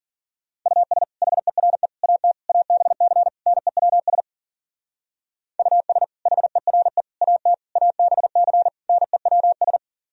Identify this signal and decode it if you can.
Morse code, heard as a keyed tone, about 51 wpm, 700 Hz. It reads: US HERE AT ABC NEWS US HERE AT ABC NEWS